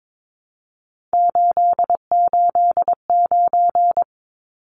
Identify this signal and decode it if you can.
Morse code, heard as a keyed tone, about 22 wpm, 700 Hz. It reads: 889
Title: Numbers - 3-Digit 22wpm